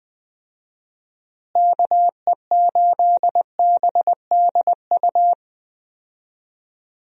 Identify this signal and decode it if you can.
Morse code, heard as a keyed tone, about 21 wpm, 700 Hz. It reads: KE8BDU
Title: USA Call Signs - 6 Characters - Repeated 3x - 20wpm